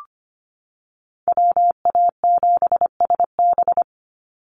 Morse code, heard as a keyed tone, about 25 wpm, 700 Hz. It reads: WA7SB
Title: NA SOTA Chaser Callsigns 25wpm